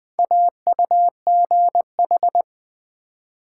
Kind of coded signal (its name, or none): Morse code